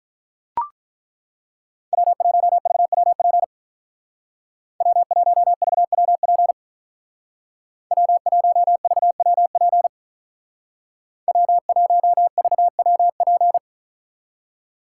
Morse code, heard as a keyed tone, about 43 wpm, 700 Hz. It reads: W1VWP W1VWP W1VWP W1VWP
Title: Speed Racing - USA 1x3 Call Signs 35wpm